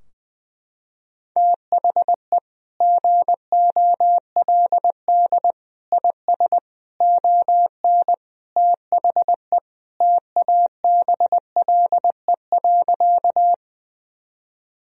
Morse code, heard as a keyed tone, about 20 wpm, 700 Hz. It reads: THE GOLD IS ON THE TABLE.
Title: Sentences from Top 1000 Words - Review Part 3 - 20wpm